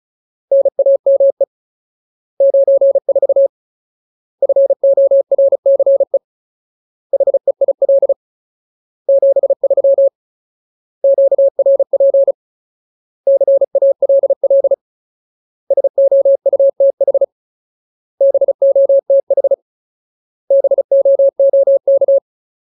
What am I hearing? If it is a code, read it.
Morse code, heard as a keyed tone, about 34 wpm, 550 Hz. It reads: NAME 94 FORCE HEIL 73 QRP CALL SOUTH BOTH BOOK
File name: Ultimate Word List / 35WPM Ultimate